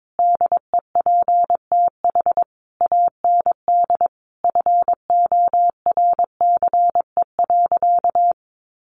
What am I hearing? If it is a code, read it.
Morse code, heard as a keyed tone, about 22 wpm, 700 Hz. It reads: DEPTH AND FORCE.
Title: Sentences from Top 1200 Words 22wpm